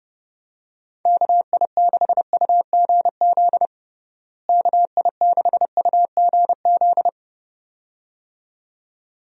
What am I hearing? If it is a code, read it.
Morse code, heard as a keyed tone, about 30 wpm, 700 Hz. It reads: KI6UGZ KI6UGZ
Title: USA Call Signs - 6 Characters - Repeated 2x - 30wpm